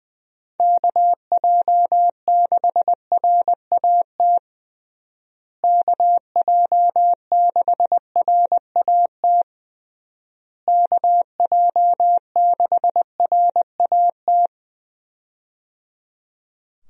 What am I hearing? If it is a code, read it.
Morse code, heard as a keyed tone, about 20 wpm, 700 Hz. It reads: KJ6RAT KJ6RAT KJ6RAT
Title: USA Call Signs - 6 Characters - Repeated 3x - 20wpm